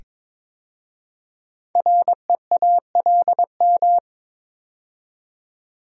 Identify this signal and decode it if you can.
Morse code, heard as a keyed tone, about 22 wpm, 700 Hz. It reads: REALM
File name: 5 Letter Words - First 3 Letters Given - 22wpm